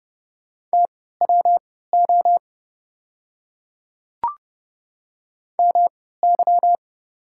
Morse code, heard as a keyed tone, about 30 wpm, spaced as for 17 wpm, 700 Hz. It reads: TWO MY
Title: 3x Character Spacing - Top 100 Words - 30wpm